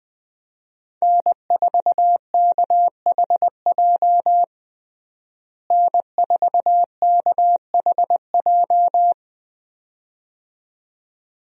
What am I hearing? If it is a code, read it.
Morse code, heard as a keyed tone, about 21 wpm, 700 Hz. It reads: N4KHJ N4KHJ
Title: USA Call Signs - 5 Characters - Repeated 2x - 20wpm